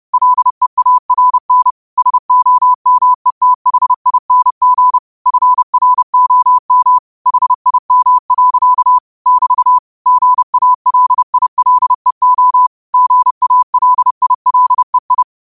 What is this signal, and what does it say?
Morse code, heard as a keyed tone, about 30 wpm, 1000 Hz. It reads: LEARN SOMETHING FROM HIM. <BT> GALILEO GALILEI
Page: New quotes every day in morse code at 30 Words per minute.